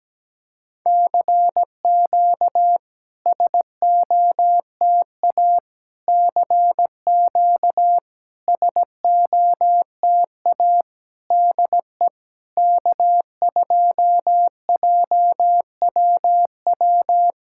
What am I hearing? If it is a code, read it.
Morse code, heard as a keyed tone, about 17 wpm, 700 Hz. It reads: CQ SOTA CQ SOTA DE K2JWW
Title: SOTA QSO Practice 17wpm